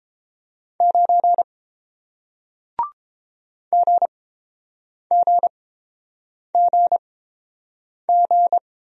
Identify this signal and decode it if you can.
Morse code, heard as a keyed tone, about 31 wpm, spaced as for 7 wpm, 700 Hz. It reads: 9 GGGG